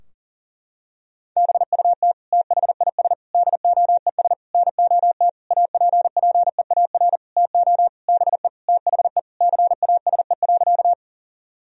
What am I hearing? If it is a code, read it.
Morse code, heard as a keyed tone, about 40 wpm, 700 Hz. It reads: BUT THIS DOES NOT APPEAR TO BE THE CASE.
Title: Sentences from Top 1000 Words - Review Part 2 - 40wpm